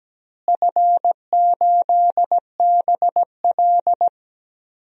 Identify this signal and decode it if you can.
Morse code, heard as a keyed tone, about 17 wpm, 700 Hz. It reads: F8BL
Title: DXCC-100 Call Signs - 4 Characters - Repeated 2x - 17wpm